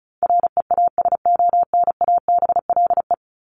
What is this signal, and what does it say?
Morse code, heard as a keyed tone, about 34 wpm, 700 Hz. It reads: REASONABLE
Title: Top 1501-1750 Words 35wpm